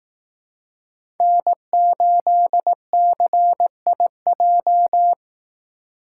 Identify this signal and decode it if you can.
Morse code, heard as a keyed tone, about 18 wpm, 700 Hz. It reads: N8CIJ
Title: Speed Racing - USA 1x3 Call Signs 15wpm